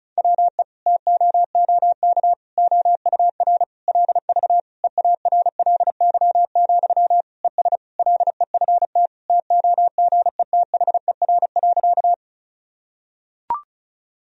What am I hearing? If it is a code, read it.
Morse code, heard as a keyed tone, about 36 wpm, 700 Hz. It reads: WE TOOK OUR LV EARLY, ES LEFT TOGETHER.
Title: Sentences from Top 400 Words and Abbreviations 35wpm